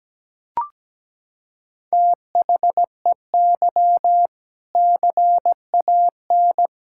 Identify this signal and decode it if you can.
Morse code, heard as a keyed tone, about 17 wpm, 700 Hz. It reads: THEY CAN
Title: Sentences from Top 1600 Words 17wpm